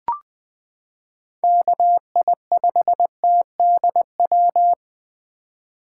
Morse code, heard as a keyed tone, about 20 wpm, 700 Hz. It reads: KI5TDW
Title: 2023 POTA Activator Call Signs - 20wpm